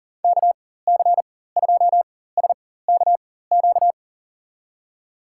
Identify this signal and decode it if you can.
Morse code, heard as a keyed tone, about 40 wpm, spaced as for 19 wpm, 700 Hz. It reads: KC2SKQ
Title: USA Call Signs - 6 Characters - 4x Character Spacing - 40wpm